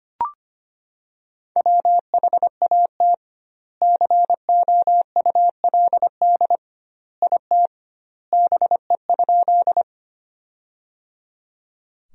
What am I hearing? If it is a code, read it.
Morse code, heard as a keyed tone, about 25 wpm, 700 Hz. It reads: WHAT COULD IT BE?
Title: 2x Word-Spacing - Sentences from Top 100 Words - 25wpm